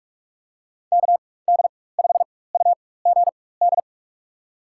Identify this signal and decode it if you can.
Morse code, heard as a keyed tone, about 46 wpm, spaced as for 21 wpm, 700 Hz. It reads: KD5UGD
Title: USA Call Signs - 6 Characters - 4x Character Spacing - 45wpm